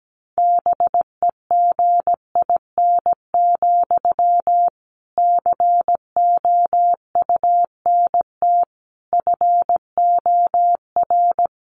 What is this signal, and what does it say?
Morse code, heard as a keyed tone, about 17 wpm, 700 Hz. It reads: BEGIN, COUNT FOR
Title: Sentences from Top 1500 Words 17wpm